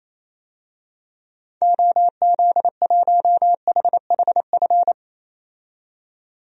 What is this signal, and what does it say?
Morse code, heard as a keyed tone, about 28 wpm, 700 Hz. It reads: OZ1HHF
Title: DXCC-100 Call Signs - 6 Characters - 28wpm